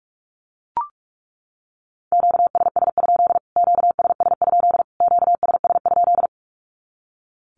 Morse code, heard as a keyed tone, about 44 wpm, 700 Hz. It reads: QSS? QSS? QSS?
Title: Q-Codes Repeated 3x 45wpm